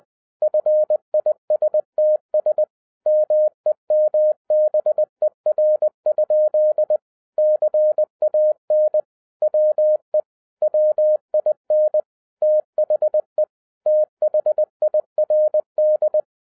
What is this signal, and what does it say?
Morse code, heard as a keyed tone, about 20 wpm, 610 Hz. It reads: FISTS MEMBER? CAN WE WIN THE THIRD